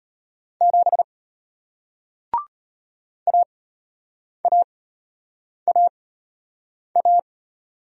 Morse code, heard as a keyed tone, about 34 wpm, spaced as for 8 wpm, 700 Hz. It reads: 7 AAAA